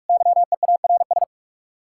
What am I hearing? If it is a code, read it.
Morse code, heard as a keyed tone, about 58 wpm, 700 Hz. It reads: YEARS